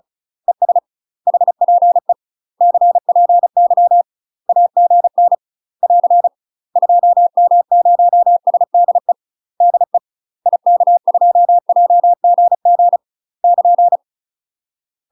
Morse code, heard as a keyed tone, about 35 wpm, 700 Hz. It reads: ES HPE CPY AGN <AR> 2M0SDE DE IK2JGG <KN>
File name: UK MASTER QSOs TRAINING / 35WPM / UK MASTER QSOs 0038